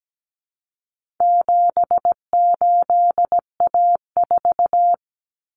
Morse code, heard as a keyed tone, about 17 wpm, 700 Hz. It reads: 78A4
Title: Common Cut Numbers - 4-Digit 17wpm